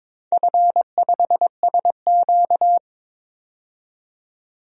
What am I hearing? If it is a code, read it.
Morse code, heard as a keyed tone, about 22 wpm, 690 Hz. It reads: F5SQ
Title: DXCC-100 Call Signs - 4 Characters - 22wpm